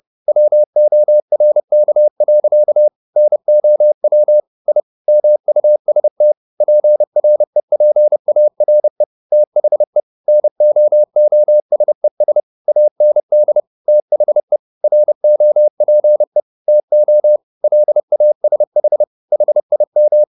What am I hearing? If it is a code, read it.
Morse code, heard as a keyed tone, about 30 wpm, 600 Hz. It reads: WORK. NOW I MUST PREPARE THE NOOSES AND THE ROPE TO LASH HIM